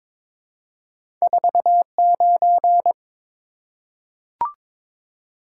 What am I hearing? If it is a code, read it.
Morse code, heard as a keyed tone, about 22 wpm, 700 Hz. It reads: 49
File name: practice / 22wpm / Two Letter-Number Combinations - 22wpm